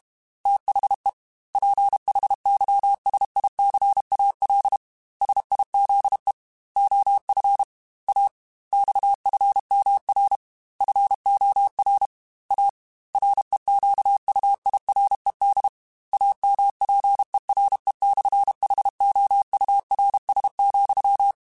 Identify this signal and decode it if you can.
Morse code, heard as a keyed tone, about 32 wpm, 800 Hz. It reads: THE PHYSICAL SIZE OF A XFMR FOR A REQUIRED AMPERE/HOURS,
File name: Technical Subjects / Technical Subjects-mp3s / Sw Pwr Sup @30